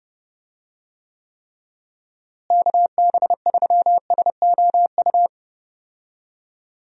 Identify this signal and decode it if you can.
Morse code, heard as a keyed tone, about 30 wpm, 700 Hz. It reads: KB3SOU
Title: USA Call Signs - 6 Characters - Repeated 3x - 30wpm